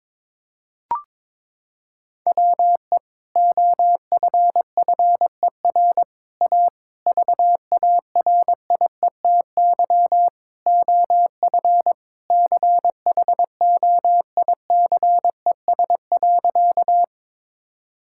Morse code, heard as a keyed tone, about 21 wpm, 700 Hz. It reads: WE OFFER A VARIETY OF CHOICES.
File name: practice / 22wpm / Sentences with 3-Word Phrases - 22wpm